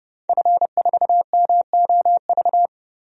Morse code, heard as a keyed tone, about 30 wpm, 700 Hz. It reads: F4MOV
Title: DXCC-100 Call Signs - 5 Characters - 30wpm